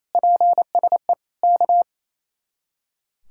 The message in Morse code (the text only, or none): PSE K